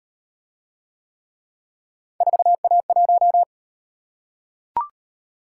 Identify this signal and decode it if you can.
Morse code, heard as a keyed tone, about 38 wpm, 700 Hz. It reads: 4A1